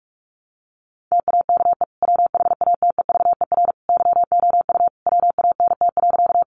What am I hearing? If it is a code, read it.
Morse code, heard as a keyed tone, about 44 wpm, 700 Hz. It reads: TAKE WHATEVER YOU WANT.
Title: Sentences from Top 1100 Words 45wpm